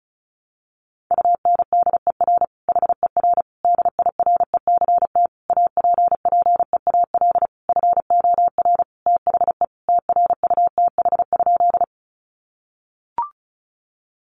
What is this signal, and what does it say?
Morse code, heard as a keyed tone, about 34 wpm, 700 Hz. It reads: UNDER HER DIRECT APPEAL FOR THE TRUTH?
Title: Sentences from Top 1300 Words 35wpm